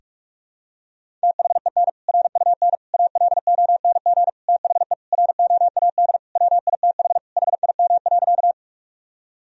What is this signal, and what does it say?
Morse code, heard as a keyed tone, about 46 wpm, 700 Hz. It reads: THEN RUN ALONG THE ROAD WITH HIM.
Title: Sentences from Top 1000 Words - Review Part 2 - 45wpm